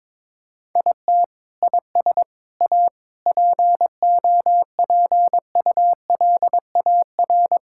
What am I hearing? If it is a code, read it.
Morse code, heard as a keyed tone, about 22 wpm, 700 Hz. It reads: IT IS A POPULAR
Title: Sentences from Top 1000 Words - Review Part 5 - 22wpm